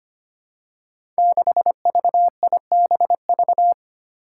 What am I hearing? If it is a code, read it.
Morse code, heard as a keyed tone, about 25 wpm, 700 Hz. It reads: 6VIBV
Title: Five Challenging Letter-Number Combinations 25wpm